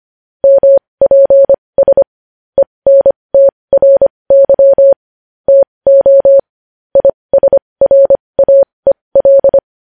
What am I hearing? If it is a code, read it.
Morse code, heard as a keyed tone, about 25 wpm, spaced as for 20 wpm, 550 Hz. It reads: MPS ENTRY TO ISRAEL
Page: News Headlines in Morse Code at 20 WPM